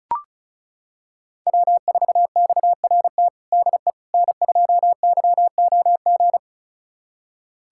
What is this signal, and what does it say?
Morse code, heard as a keyed tone, about 36 wpm, 700 Hz. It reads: W4XRT DE N2YOG
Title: USA Callsign DE Callsign 35wpm